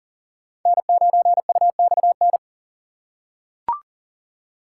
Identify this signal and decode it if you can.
Morse code, heard as a keyed tone, about 40 wpm, 700 Hz. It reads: N9UXN
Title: USA Call Signs - 5 Characters - 6x Character Spacing - 40wpm